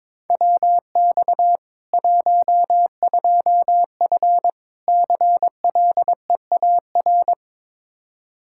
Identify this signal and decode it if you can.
Morse code, heard as a keyed tone, about 21 wpm, 700 Hz. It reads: WX 12F CLEAR
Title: Practice QSO - Exchange 2 - Info - 22wpm